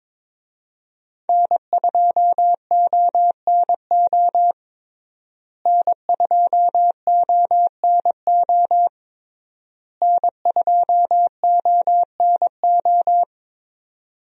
Morse code, heard as a keyed tone, about 22 wpm, 700 Hz. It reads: N2ONO N2ONO N2ONO
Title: USA Call Signs - 5 Characters - Repeated 3x - 22wpm